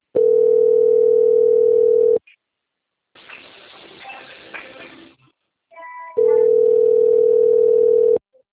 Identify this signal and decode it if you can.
Ringback tone (North America), 440 + 480 Hz.